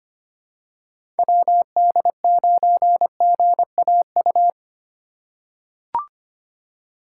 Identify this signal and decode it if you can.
Morse code, heard as a keyed tone, about 25 wpm, 700 Hz. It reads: WD9GAU
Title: USA Call Signs - 6 Characters - 4x Character Spacing - 25wpm